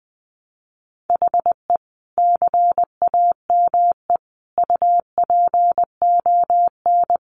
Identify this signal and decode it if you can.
Morse code, heard as a keyed tone, about 20 wpm, 700 Hz. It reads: HE CAME UPON
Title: Sentences from Top 1100 Words 20wpm